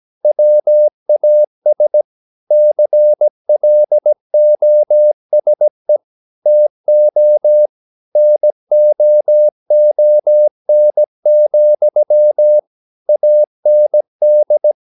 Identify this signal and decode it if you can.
Morse code, heard as a keyed tone, about 17 wpm, 600 Hz. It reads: WAS CLOSE TO NOON, AND
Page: Romanen Never af Ken Follet på engelsk.